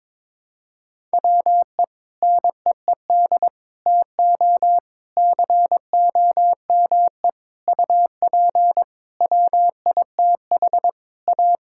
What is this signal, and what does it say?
Morse code, heard as a keyed tone, about 22 wpm, 700 Hz. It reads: WE NEED TO COME UP WITH A
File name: Sentences with 7-Word Phrases - 22wpm